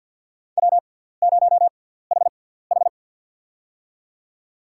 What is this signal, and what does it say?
Morse code, heard as a keyed tone, about 48 wpm, spaced as for 17 wpm, 700 Hz. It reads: W0HH
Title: USA Call Signs - 4 Characters - 6x Character Spacing - 50wpm